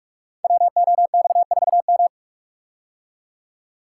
Morse code, heard as a keyed tone, about 44 wpm, 690 Hz. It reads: WOX4M
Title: Code Groups - 5 Letter:Numbers - 45wpm